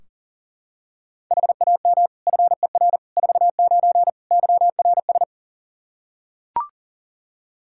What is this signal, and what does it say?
Morse code, heard as a keyed tone, about 40 wpm, 700 Hz. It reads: HAM FER 49 YRS
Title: K1USN SST QSO Practice 40wpm